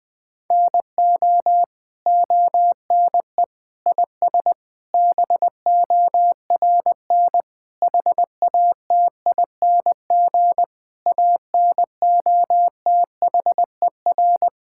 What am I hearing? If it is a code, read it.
Morse code, heard as a keyed tone, about 20 wpm, 700 Hz. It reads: NO ONE IS BORN HATING ANOTHER
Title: Sets of 9 Words - 20wpm